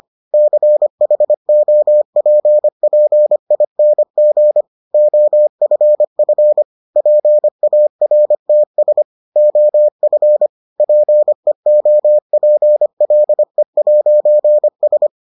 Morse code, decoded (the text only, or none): CHOPPING OFF PARTS OF PEOPLE'S